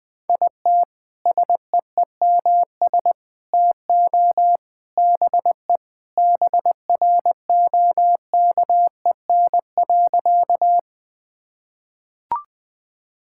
Morse code, heard as a keyed tone, about 21 wpm, 700 Hz. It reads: IT SEEMS TO BE BROKEN.
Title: Sentences with 3-Word Phrases - Spoken Beginning - 20wpm